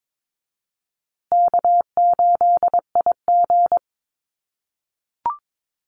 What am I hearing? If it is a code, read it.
Morse code, heard as a keyed tone, about 22 wpm, 700 Hz. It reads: K8IG
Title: USA Call Signs Repeated 3x 22wpm